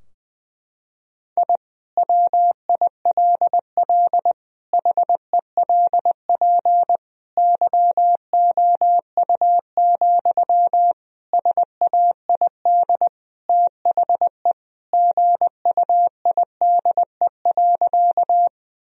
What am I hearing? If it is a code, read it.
Morse code, heard as a keyed tone, about 20 wpm, 700 Hz. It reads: I WILL HELP YOU, SAID THE GUIDE.
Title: Sentences from Top 1000 Words - Review Part 3 - 20wpm